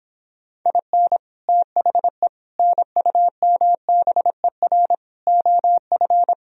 Morse code, heard as a keyed tone, about 26 wpm, 700 Hz. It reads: IN THE NUMBER OF